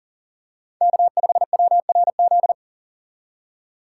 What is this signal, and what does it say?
Morse code, heard as a keyed tone, about 39 wpm, 700 Hz. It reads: K5WRZ
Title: USA Call Signs - 5 Characters - 2x Character Spacing - 40wpm